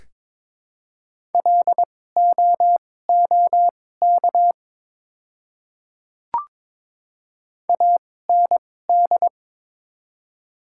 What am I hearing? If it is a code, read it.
Morse code, heard as a keyed tone, about 21 wpm, spaced as for 16 wpm, 700 Hz. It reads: LOOK AND
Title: 2x Character Spacing - Top 100 Words - 22wpm